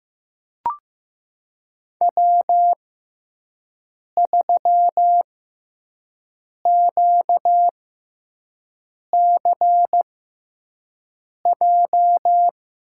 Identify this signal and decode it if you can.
Morse code, heard as a keyed tone, about 15 wpm, spaced as for 5 wpm, 700 Hz. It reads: W3QCJ
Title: USA Call Signs - 5 Characters - 6x Character Spacing - 15wpm